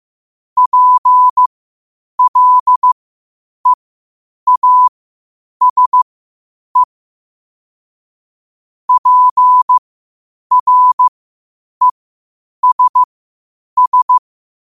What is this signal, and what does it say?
Morse code, heard as a keyed tone, about 15 wpm, spaced as for 8 wpm, 1000 Hz. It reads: PLEASE PRESS
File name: qotdfiles / 8wpm